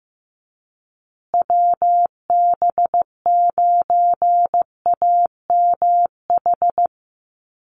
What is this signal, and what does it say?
Morse code, heard as a keyed tone, about 15 wpm, 700 Hz. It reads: WB9AMH
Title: USA Call Signs Repeated 3x 15wpm